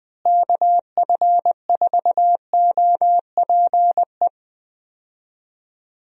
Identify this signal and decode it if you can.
Morse code, heard as a keyed tone, about 20 wpm, 700 Hz. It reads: KF4OPE
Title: USA Call Signs - 6 Characters - Repeated 2x - 20wpm